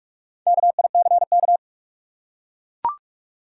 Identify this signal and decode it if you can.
Morse code, heard as a keyed tone, about 43 wpm, 700 Hz. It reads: KICK
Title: Four Letter Words - 45wpm